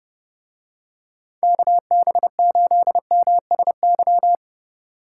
Morse code, heard as a keyed tone, about 29 wpm, 700 Hz. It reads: KB8MSY